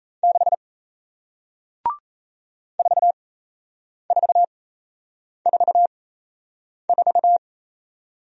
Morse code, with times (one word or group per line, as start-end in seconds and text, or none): 0.2-0.5 6
2.8-7.4 4444